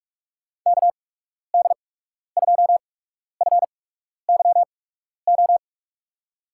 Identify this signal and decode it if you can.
Morse code, heard as a keyed tone, about 42 wpm, spaced as for 12 wpm, 700 Hz. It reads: KD2FYO
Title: USA Call Signs - 6 Characters - 8x Character Spacing - 45wpm